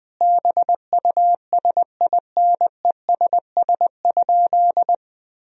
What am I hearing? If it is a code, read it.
Morse code, heard as a keyed tone, about 20 wpm, 700 Hz. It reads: BUSINESS?
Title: Sentences from Top 1000 Words - Review Part 3 - 20wpm